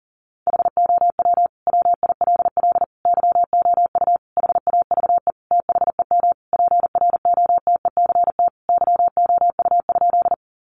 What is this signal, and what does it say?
Morse code, heard as a keyed tone, about 40 wpm, 700 Hz. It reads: HOW WILL YOU HAVE THEM PROTECT YOU?
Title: Sentences from Top 900 Words 40wpm